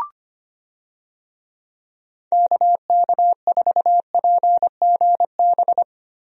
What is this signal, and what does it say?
Morse code, heard as a keyed tone, about 25 wpm, 700 Hz. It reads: KK4PGB
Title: USA Call Signs - 6 Characters - Repeated 2x - 25wpm